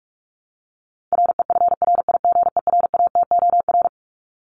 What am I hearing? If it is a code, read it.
Morse code, heard as a keyed tone, about 44 wpm, 700 Hz. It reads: REFRIGERATOR
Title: Top 1501-1750 Words 45wpm